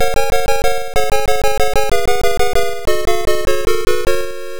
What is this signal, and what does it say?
There is no code.